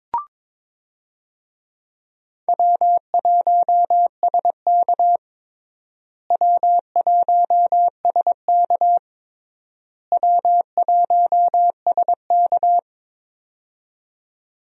Morse code, heard as a keyed tone, about 22 wpm, 700 Hz. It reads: W1SK W1SK W1SK
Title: USA Call Signs - 4 Characters - Repeated 3x - 22wpm